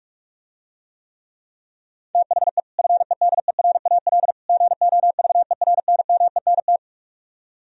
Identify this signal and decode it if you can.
Morse code, heard as a keyed tone, about 44 wpm, 700 Hz. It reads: THE FEDERAL GOVERNMENT